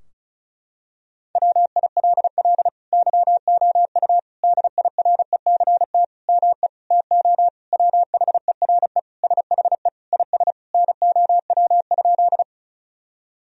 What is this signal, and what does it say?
Morse code, heard as a keyed tone, about 34 wpm, 700 Hz. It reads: WILL YOU DIRECT ME TO WHERE SHE IS NOW?
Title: Sentences from Top 1000 Words - Review Part 2 - 35wpm